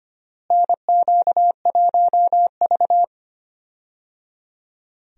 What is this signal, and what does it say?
Morse code, heard as a keyed tone, about 25 wpm, 700 Hz. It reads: NQ1V
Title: USA Call Signs - 4 Characters - Repeated 3x - 25wpm